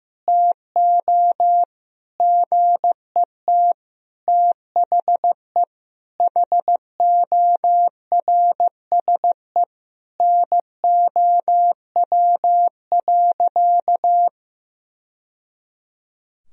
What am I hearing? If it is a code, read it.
Morse code, heard as a keyed tone, about 16 wpm, 700 Hz. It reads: TO GET THE HORSE NOW.
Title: Sentences from Top 1000 Words - Review Part 1 - 15wpm